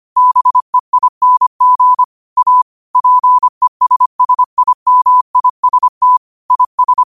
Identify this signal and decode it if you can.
Morse code, heard as a keyed tone, about 25 wpm, 1000 Hz. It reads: BEING A PESSIMIST IS
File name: qotdfiles / 25wpm